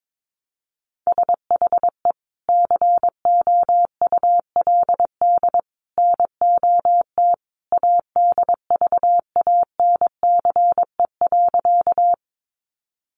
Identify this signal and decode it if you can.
Morse code, heard as a keyed tone, about 22 wpm, 700 Hz. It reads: SHE COULD NOT ADVANCE.
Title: Sentences from Top 1400 Words 22wpm